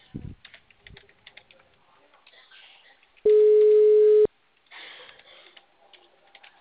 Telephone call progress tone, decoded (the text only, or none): ringback tone